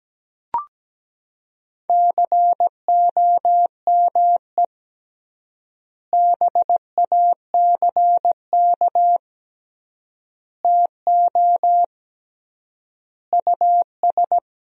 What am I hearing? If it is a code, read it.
Morse code, heard as a keyed tone, about 17 wpm, 700 Hz. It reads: COME BACK TO US
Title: 3x Word-Spacing - Sentences from Top 500 Words - 17wpm